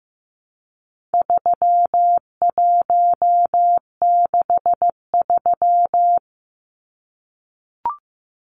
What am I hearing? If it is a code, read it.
Morse code, heard as a keyed tone, about 15 wpm, 700 Hz. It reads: 3163
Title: Numbers - 4-Digit 15wpm